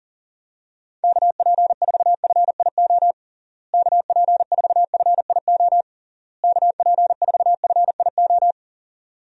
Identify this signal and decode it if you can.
Morse code, heard as a keyed tone, about 37 wpm, 700 Hz. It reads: KP4FIO KP4FIO KP4FIO
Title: USA Call Signs - 6 Characters - Repeated 3x - 40wpm